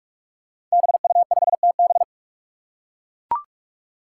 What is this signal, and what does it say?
Morse code, heard as a keyed tone, about 44 wpm, 700 Hz. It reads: BU5TB
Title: Five Challenging Letter-Number Combinations 45wpm